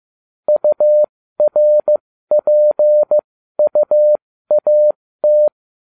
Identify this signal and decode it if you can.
Morse code, heard as a keyed tone, about 15 wpm, spaced as for 13 wpm, 600 Hz. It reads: URPUAT